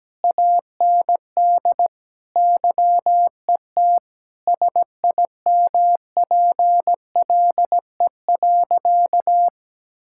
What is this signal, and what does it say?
Morse code, heard as a keyed tone, about 17 wpm, 700 Hz. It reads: AND YET SIMPLE.
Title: Sentences from Top 1000 Words - Review Part 6 - 17wpm